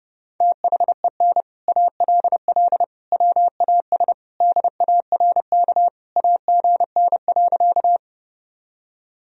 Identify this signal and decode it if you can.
Morse code, heard as a keyed tone, about 32 wpm, 700 Hz. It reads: THEN ALL WAS DARK AGN.
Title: Sentences from Top 500 Words and Abbreviations 30wpm